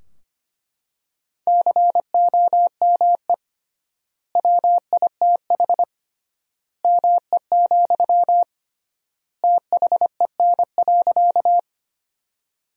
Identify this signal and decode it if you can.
Morse code, heard as a keyed tone, about 25 wpm, 700 Hz. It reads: COME WITH ME, THEN.
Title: 3x Word-Spacing - Sentences from Top 200 Words - 25wpm